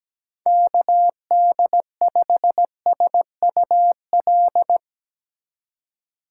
Morse code, heard as a keyed tone, about 18 wpm, 700 Hz. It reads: KD5SUL
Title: Speed Racing - USA 2x3 Call Signs 17wpm